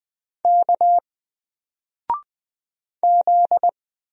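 Morse code, heard as a keyed tone, about 20 wpm, 700 Hz. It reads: K Z
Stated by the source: Single letter and number Morse drill at 20 wpm, from N4LCD.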